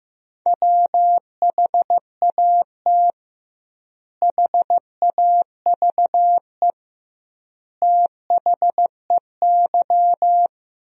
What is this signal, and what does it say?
Morse code, heard as a keyed tone, about 15 wpm, 700 Hz. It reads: WHAT HAVE THEY
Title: 2x Word-Spacing - Sentences from Top 500 Words - 15wpm